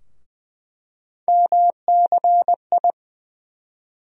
Morse code, heard as a keyed tone, about 20 wpm, 700 Hz. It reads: MCI
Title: USA Call Sign Postfixes - 3 Letters 20wpm